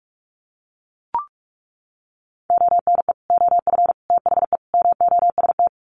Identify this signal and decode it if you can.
Morse code, heard as a keyed tone, about 44 wpm, 700 Hz. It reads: ONE OF THE MOST
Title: Sets of 4 Words 45wpm